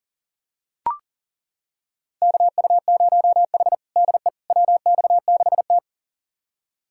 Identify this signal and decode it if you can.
Morse code, heard as a keyed tone, about 40 wpm, 700 Hz. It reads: KU0H DE WX6T
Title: USA Callsign DE Callsign - 4 Characters - 40wpm